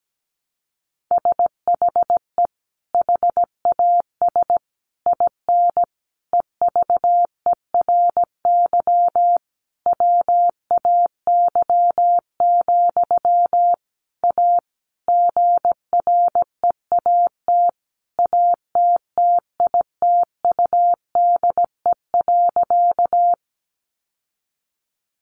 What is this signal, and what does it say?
Morse code, heard as a keyed tone, about 17 wpm, 700 Hz. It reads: SHE HAS IN EVERY WAY, A GREAT ATTITUDE.
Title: Sentences from Top 1300 Words 17wpm